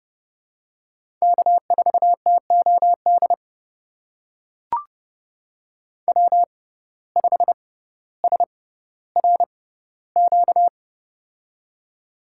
Morse code, heard as a keyed tone, about 29 wpm, spaced as for 15 wpm, 700 Hz. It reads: K4TOD W5SRQ
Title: USA Call Signs - 5 Characters - 6x Character Spacing - 30wpm